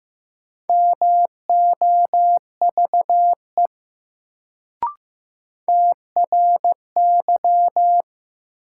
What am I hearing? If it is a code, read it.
Morse code, heard as a keyed tone, about 15 wpm, 700 Hz. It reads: MOVE TRY
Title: Sentences from Top 200 Words - New Word Drill - 15wpm